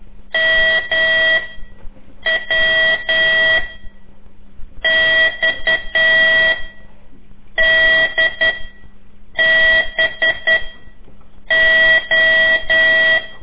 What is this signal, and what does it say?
Morse code, heard as a keyed tone, about 11 wpm, 1950 Hz. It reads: M W X D B O
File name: CORSI CW